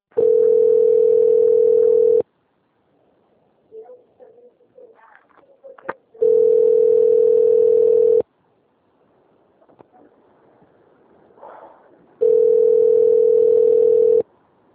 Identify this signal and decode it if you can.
Ringback tone (North America), 440 + 480 Hz.